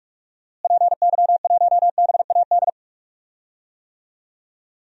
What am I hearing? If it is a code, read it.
Morse code, heard as a keyed tone, about 46 wpm, 700 Hz. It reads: PY1BAD
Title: DXCC-100 Call Signs - 6 Characters - Repeated 3x - 45wpm